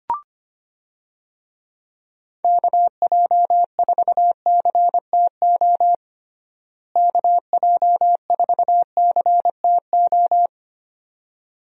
Morse code, heard as a keyed tone, about 25 wpm, 700 Hz. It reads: KJ4CTO KJ4CTO
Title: USA Call Signs - 6 Characters - Repeated 2x - 25wpm